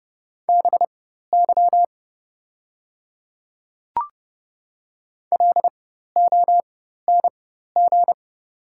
Morse code, heard as a keyed tone, about 30 wpm, spaced as for 14 wpm, 700 Hz. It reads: BY LONG
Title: 4x Character Spacing - Top 200 Words - 30wpm